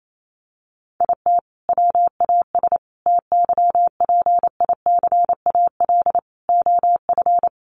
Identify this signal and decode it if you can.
Morse code, heard as a keyed tone, about 28 wpm, 700 Hz. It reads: IT WAS TYPICAL OF
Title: Sentences from Top 1100 Words 28wpm